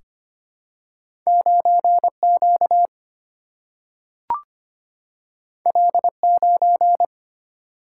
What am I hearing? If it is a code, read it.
Morse code, heard as a keyed tone, about 25 wpm, 700 Hz. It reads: 9Q L9
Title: Single-Letter Single-Number 25wpm